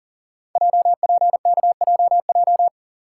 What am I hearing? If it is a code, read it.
Morse code, heard as a keyed tone, about 40 wpm, 700 Hz. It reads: JPKJJ
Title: Code Groups - 5 Letter:Numbers - 40wpm